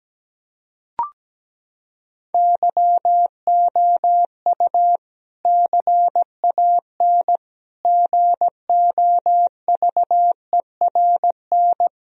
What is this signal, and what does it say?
Morse code, heard as a keyed tone, about 17 wpm, 700 Hz. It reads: YOU CAN GOVERN
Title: Sentences from Top 1000 Words - Review Part 2 - 17wpm